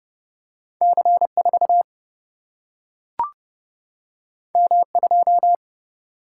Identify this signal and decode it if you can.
Morse code, heard as a keyed tone, about 30 wpm, 700 Hz. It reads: C 4 M 2
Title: Single-Letter Single-Number 30wpm